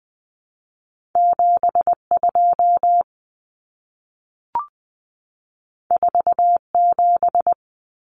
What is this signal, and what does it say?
Morse code, heard as a keyed tone, about 20 wpm, 700 Hz. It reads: 72 47
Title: Common Cut Numbers - 2-Digit 20wpm